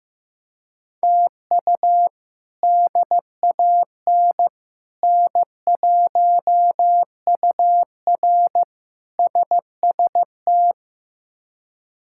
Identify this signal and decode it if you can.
Morse code, heard as a keyed tone, about 16 wpm, 700 Hz. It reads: TU DAN N1UR SST
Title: K1USN SST QSO Practice - 15wpm